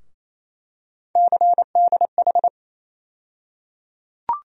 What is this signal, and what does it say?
Morse code, heard as a keyed tone, about 28 wpm, 700 Hz. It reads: CDH
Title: USA Call Sign Postfixes - 3 Letters 28wpm